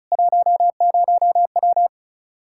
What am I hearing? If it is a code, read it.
Morse code, heard as a keyed tone, about 34 wpm, 700 Hz. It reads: 10W
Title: Practice QSO - Exchange 2 - Info - 35wpm